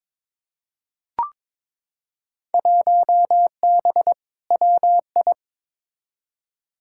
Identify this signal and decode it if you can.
Morse code, heard as a keyed tone, about 23 wpm, 700 Hz. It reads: 1B WI
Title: Field Day Exchanges - 22wpm